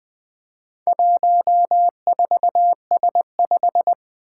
Morse code, heard as a keyed tone, about 20 wpm, 700 Hz. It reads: 14S5
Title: Four Challenging Letter-Number Combinations 20wpm